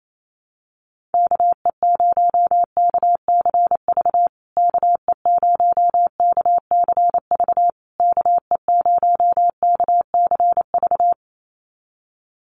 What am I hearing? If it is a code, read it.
Morse code, heard as a keyed tone, about 28 wpm, 700 Hz. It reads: KE0KCV KE0KCV KE0KCV
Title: USA Call Signs Repeated 3x 28wpm